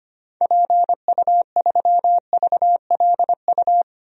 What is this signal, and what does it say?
Morse code, heard as a keyed tone, about 25 wpm, 700 Hz. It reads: PU3VLU
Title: DXCC-100 Call Signs - 6 Characters - Repeated 2x - 25wpm